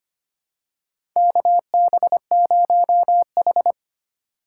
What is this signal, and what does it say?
Morse code, heard as a keyed tone, about 25 wpm, 700 Hz. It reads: KB0H
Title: USA Call Signs - 3x Character Spacing - 25wpm